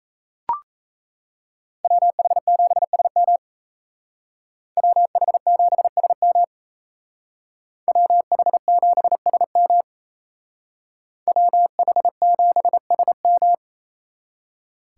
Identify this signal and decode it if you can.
Morse code, heard as a keyed tone, about 37 wpm, 700 Hz. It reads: WH7SM WH7SM WH7SM WH7SM